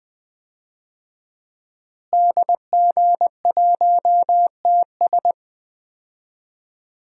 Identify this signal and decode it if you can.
Morse code, heard as a keyed tone, about 20 wpm, 700 Hz. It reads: DG1TS
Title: DXCC-100 Call Signs - 5 Characters - Repeated 2x - 20wpm